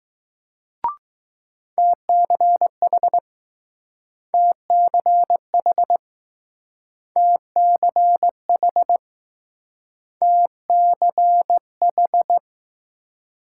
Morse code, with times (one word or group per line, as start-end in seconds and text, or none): 1.8-3.2 TCH
4.3-6.0 TCH
7.2-9.0 TCH
10.2-12.4 TCH